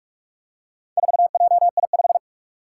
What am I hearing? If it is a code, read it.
Morse code, heard as a keyed tone, about 46 wpm, 700 Hz. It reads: 4JI5
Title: Four Challenging Letter-Number Combinations 45wpm